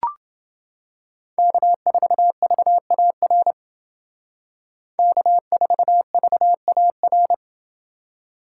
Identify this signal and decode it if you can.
Morse code, heard as a keyed tone, about 29 wpm, 700 Hz. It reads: K4VAR K4VAR